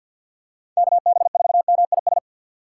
Morse code, heard as a keyed tone, about 53 wpm, 700 Hz. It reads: KB4MIS
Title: USA Call Signs - 6 Characters - 6x Character Spacing - 50wpm